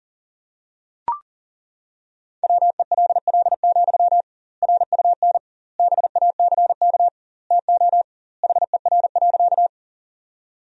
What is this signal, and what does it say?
Morse code, heard as a keyed tone, about 40 wpm, 700 Hz. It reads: WELL, RUN BACK TO HER.
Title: 2x Word-Spacing - Sentences from Top 300 Words - 40wpm